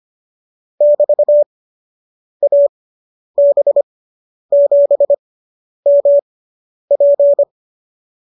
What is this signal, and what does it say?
Morse code, heard as a keyed tone, about 25 wpm, spaced as for 9 wpm, 570 Hz. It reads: <BT>AB7MP